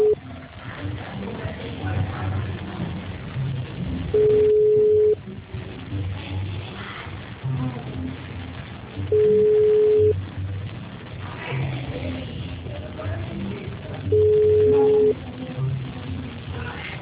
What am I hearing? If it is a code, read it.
Ringback tone (Europe, 425 Hz).